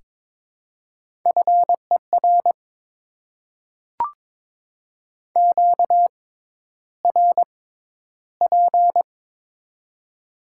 Morse code, heard as a keyed tone, about 22 wpm, spaced as for 11 wpm, 700 Hz. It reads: FER QRP
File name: Top 90 QSO Elements - 6x Character Spacing - 22wpm